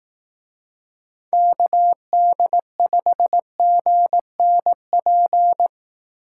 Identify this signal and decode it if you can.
Morse code, heard as a keyed tone, about 18 wpm, 700 Hz. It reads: KD5GNP
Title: Speed Racing - USA 2x3 Call Signs 15wpm